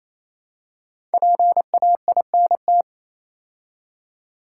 Morse code, heard as a keyed tone, about 28 wpm, 700 Hz. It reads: PAINT
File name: Easy Words - Speed-Racing with Hints - 22wpm